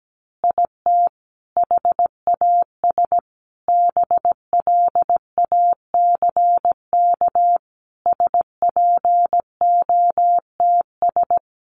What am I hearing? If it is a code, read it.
Morse code, heard as a keyed tone, about 17 wpm, 700 Hz. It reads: IT HAS BLACK SPOTS
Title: Additional Easy Phrases 17wpm